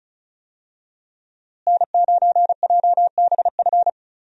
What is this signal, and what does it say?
Morse code, heard as a keyed tone, about 36 wpm, 700 Hz. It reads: N9JBF
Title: USA Call Signs - 5 Characters - Repeated 2x - 35wpm